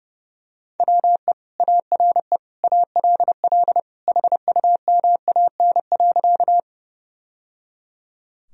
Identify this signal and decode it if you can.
Morse code, heard as a keyed tone, about 30 wpm, 700 Hz. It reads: WE ARE ALL HUMAN.
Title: Sentences from Top 1000 Words - Review Part 3 - 30wpm